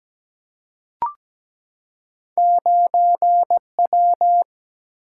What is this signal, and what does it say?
Morse code, heard as a keyed tone, about 17 wpm, 700 Hz. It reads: 9W
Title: Single-Letter Single-Number 17wpm